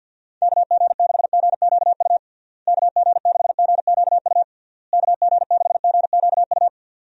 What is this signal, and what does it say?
Morse code, heard as a keyed tone, about 50 wpm, 700 Hz. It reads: KG6GQU KG6GQU KG6GQU
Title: USA Call Signs - 6 Characters - Repeated 3x - 50wpm